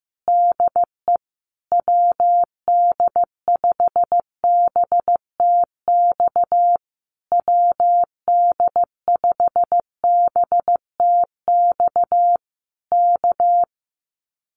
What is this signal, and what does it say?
Morse code, heard as a keyed tone, about 15 wpm, 700 Hz. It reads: DE WD5BTX WD5BTX K
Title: Common QSO Phrases Encore 15wpm